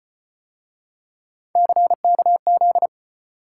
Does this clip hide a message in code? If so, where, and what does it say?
Yes: Morse code from 1.6 to 2.9 s, heard as a keyed tone, about 32 wpm, 700 Hz. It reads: CKZ